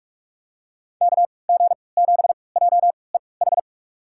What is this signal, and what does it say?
Morse code, heard as a keyed tone, about 48 wpm, spaced as for 26 wpm, 700 Hz. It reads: KG7JEH
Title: USA Call Signs - 6 Characters - 3x Character Spacing - 45wpm